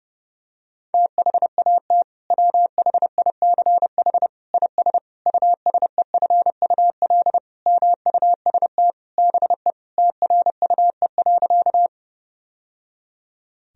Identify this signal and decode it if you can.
Morse code, heard as a keyed tone, about 30 wpm, 700 Hz. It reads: THAT WHICH IS USEFUL MUST BE TRUE.
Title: Sentences from Top 1000 Words - Review Part 4 - 30wpm